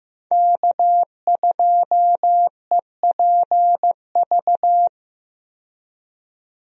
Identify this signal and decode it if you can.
Morse code, heard as a keyed tone, about 15 wpm, 700 Hz. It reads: K2EPV
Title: USA Call Signs - 5 Characters - 15wpm